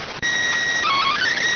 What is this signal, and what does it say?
SSTV (slow-scan television), PD 290.